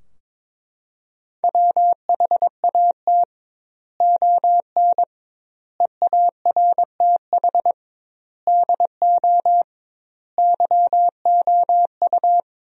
Morse code, heard as a keyed tone, about 22 wpm, 700 Hz. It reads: WHAT ON EARTH DO YOU
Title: 2x Word-Spacing - Sentences from Top 200 Words - 22wpm